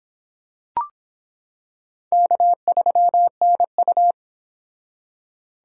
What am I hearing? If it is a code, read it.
Morse code, heard as a keyed tone, about 25 wpm, 700 Hz. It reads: K3NU